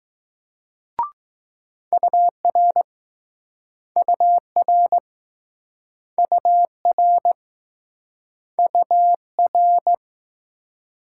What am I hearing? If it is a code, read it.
Morse code, heard as a keyed tone, about 20 wpm, 700 Hz. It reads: UR UR UR UR